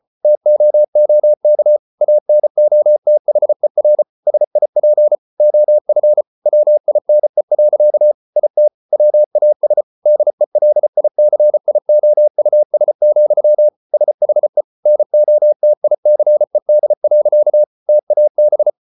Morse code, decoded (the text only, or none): TOOK ANOTHER SIP OF WINE. IT WAS DELICIOUS, SHE NOTICED. TAB